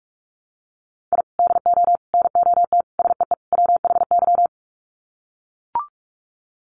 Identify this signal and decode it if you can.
Morse code, heard as a keyed tone, about 44 wpm, 700 Hz. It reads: I DO NOT SEE WHY
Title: Sets of 5 Words Encore 45wpm